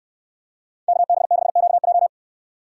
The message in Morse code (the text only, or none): KB6QJ